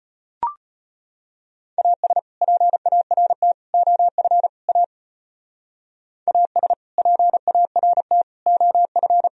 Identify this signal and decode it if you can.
Morse code, heard as a keyed tone, about 30 wpm, 700 Hz. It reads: AS PART OF A AS PART OF
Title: Speed-Racing - Sets of 4 Words 25wpm